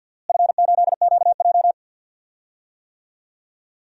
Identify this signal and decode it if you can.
Morse code, heard as a keyed tone, about 50 wpm, 700 Hz. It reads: F8QJ
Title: DXCC-100 Call Signs - 4 Characters - Repeated 3x - 50wpm